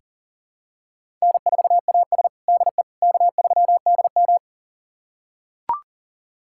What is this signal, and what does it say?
Morse code, heard as a keyed tone, about 40 wpm, 700 Hz. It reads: N4AS DE K3DM
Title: USA Callsign DE Callsign - 4 Characters - 40wpm